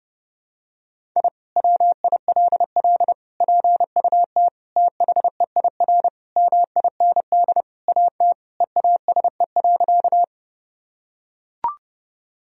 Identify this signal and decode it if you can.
Morse code, heard as a keyed tone, about 32 wpm, 700 Hz. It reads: I WILL PUT THEIR MIND AT EASE.
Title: Sentences from Top 1000 Words - Review Part 1 - 30wpm